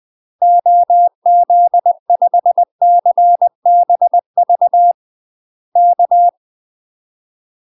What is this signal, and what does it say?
Morse code, heard as a keyed tone, about 20 wpm, 700 Hz. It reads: OZ5CBV K
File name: UK MASTER QSOs TRAINING/20WPM